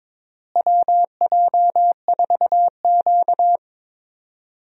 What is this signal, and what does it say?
Morse code, heard as a keyed tone, about 22 wpm, 700 Hz. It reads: WJ4Q
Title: USA Call Signs - 4 Characters - Repeated 3x - 22wpm